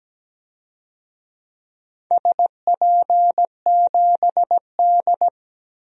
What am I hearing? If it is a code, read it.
Morse code, heard as a keyed tone, about 17 wpm, 700 Hz. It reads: SP7D
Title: DXCC-100 Call Signs - 4 Characters - 17wpm